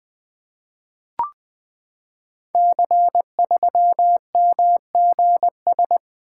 Morse code, heard as a keyed tone, about 20 wpm, 700 Hz. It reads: C3MGS
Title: Code Groups - 5 Letter:Numbers - 20wpm